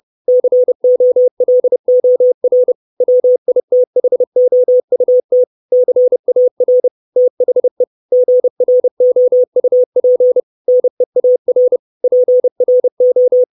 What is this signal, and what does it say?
Morse code, heard as a keyed tone, about 30 wpm, 500 Hz. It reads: COLOR WITHOUT CAR THE GROUP NEAR PRO
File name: 632 Words 30WPM